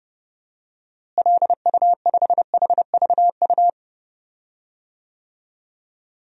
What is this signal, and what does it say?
Morse code, heard as a keyed tone, about 30 wpm, 700 Hz. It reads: LU5HVU
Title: DXCC-100 Call Signs - 6 Characters - Repeated 2x - 30wpm